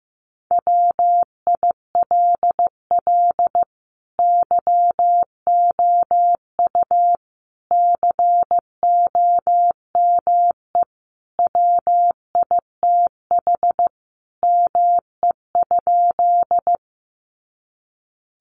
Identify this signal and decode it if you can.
Morse code, heard as a keyed tone, about 15 wpm, 700 Hz. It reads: WILL YOU COME WITH ME?
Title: Sentences from Top 200 Words 15wpm